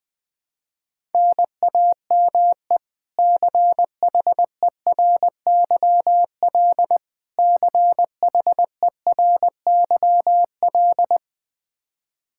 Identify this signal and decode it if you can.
Morse code, heard as a keyed tone, about 21 wpm, 700 Hz. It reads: NAME CHERYL CHERYL
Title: K1USN SST QSO Practice 20wpm